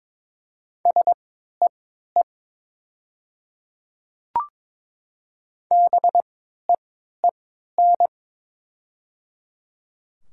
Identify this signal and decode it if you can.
Morse code, heard as a keyed tone, about 22 wpm, spaced as for 13 wpm, 690 Hz. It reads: SEE BEEN
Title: 3x Character Spacing - Top 100 Words - 22wpm